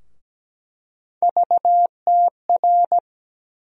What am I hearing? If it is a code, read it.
Morse code, heard as a keyed tone, about 17 wpm, 700 Hz. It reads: VTR
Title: USA Call Sign Postfixes - 3 Letters 17wpm